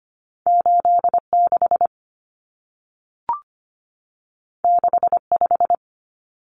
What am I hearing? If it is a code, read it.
Morse code, heard as a keyed tone, about 25 wpm, 700 Hz. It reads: 86 65
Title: Numbers - 2-Digit 25wpm